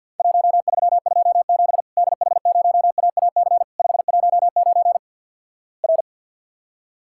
Morse code, heard as a keyed tone, about 53 wpm, 700 Hz. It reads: 1327 DH0AAC 519 R
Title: Worked All Europe DX Contest - QTC Exchange with Repeats - 50wpm